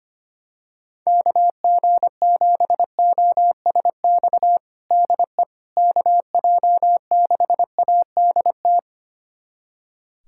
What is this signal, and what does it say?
Morse code, heard as a keyed tone, about 25 wpm, 700 Hz. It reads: KG7OSX DE KJ6ADT
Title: USA Callsign DE Callsign 25wpm